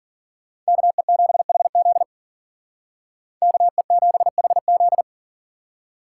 Morse code, heard as a keyed tone, about 50 wpm, 700 Hz. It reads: KE7HZ KE7HZ
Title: Speed Racing - USA 2x2 Call Signs 40wpm